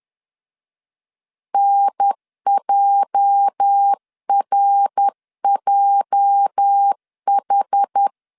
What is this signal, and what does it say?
Morse code, heard as a keyed tone, about 10 wpm, 800 Hz. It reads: NJRJH
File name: LEZIONI / 8 YJ45 / 8 YJ 10wpm 2651